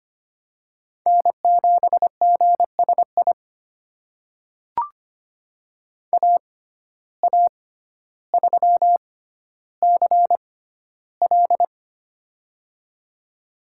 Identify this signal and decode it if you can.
Morse code, heard as a keyed tone, about 25 wpm, spaced as for 13 wpm, 700 Hz. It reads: N7GSI AA3CL
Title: USA Call Signs - 5 Characters - 6x Character Spacing - 25wpm